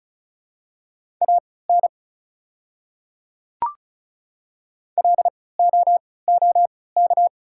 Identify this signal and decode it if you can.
Morse code, heard as a keyed tone, about 34 wpm, spaced as for 20 wpm, 690 Hz. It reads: AN LOOK
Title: 3x Character Spacing - Top 100 Words - 35wpm